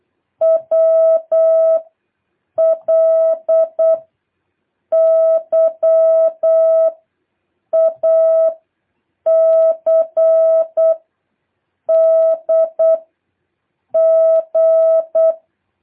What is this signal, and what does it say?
Morse code, heard as a keyed tone, about 8 wpm, 650 Hz. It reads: W L Y A C D G
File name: jh8cbh / 2023cw